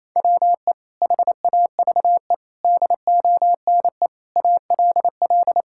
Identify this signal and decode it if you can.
Morse code, heard as a keyed tone, about 28 wpm, 700 Hz. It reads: WE HAVE DONE ALL
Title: Sentences from Top 1000 Words - Review Part 5 - 28wpm